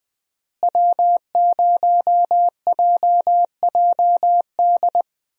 Morse code, heard as a keyed tone, about 21 wpm, 700 Hz. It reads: W0JJD
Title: USA Call Signs - 5 Characters - Repeated 3x - 20wpm